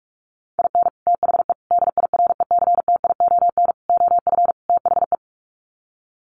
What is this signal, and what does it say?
Morse code, heard as a keyed tone, about 44 wpm, 700 Hz. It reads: IN THE DIRECTION OF THE
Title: Sets of 5 Words 45wpm